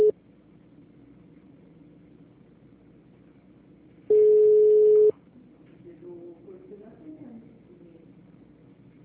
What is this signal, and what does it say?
Ringback tone (Europe, 425 Hz), with speech.